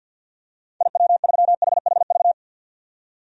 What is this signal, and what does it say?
Morse code, heard as a keyed tone, about 54 wpm, 700 Hz. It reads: IW3HHV
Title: DXCC-100 Call Signs - 6 Characters - 50wpm